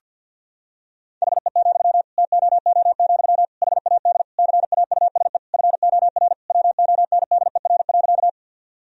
Morse code, heard as a keyed tone, about 50 wpm, 700 Hz. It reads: HE, TOO, HAD CAUSE FOR WONDER.
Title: Sentences from Top 1000 Words - Review Part 3 - 50wpm